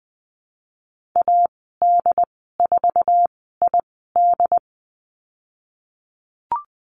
Morse code, heard as a keyed tone, about 20 wpm, spaced as for 14 wpm, 700 Hz. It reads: AD4ID
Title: 076 Lesson 13g - Letter L - Call Signs 20wpm